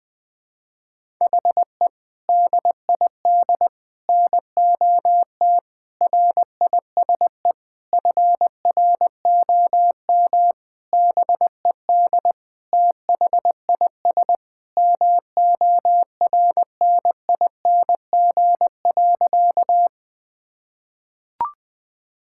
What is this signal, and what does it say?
Morse code, heard as a keyed tone, about 20 wpm, 700 Hz. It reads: HE DID NOT RISE FROM BED THIS MORNING.
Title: Sentences from Top 1000 Words - Review Part 6 - 20wpm